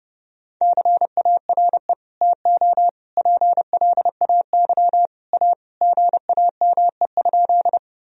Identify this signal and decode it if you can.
Morse code, heard as a keyed tone, about 32 wpm, 700 Hz. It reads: CARE TO PLAY A GAME?
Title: Sentences from Top 1000 Words - Review Part 1 - 30wpm